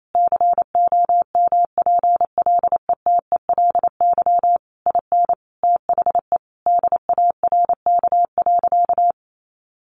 Morse code, heard as a keyed tone, about 28 wpm, 700 Hz. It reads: COMPLETELY IN THE DARK.
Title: Sentences from Top 1100 Words 28wpm